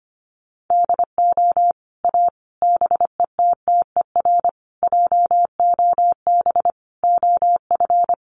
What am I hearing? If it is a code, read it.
Morse code, heard as a keyed tone, about 25 wpm, 700 Hz. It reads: DO A BETTER JOB OF
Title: Sets of 5 Words Encore2 25wpm